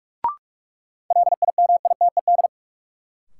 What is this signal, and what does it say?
Morse code, heard as a keyed tone, about 45 wpm, 700 Hz. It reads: LIMITED
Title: Sentences from Top 900 Words - New Word Drill - 45wpm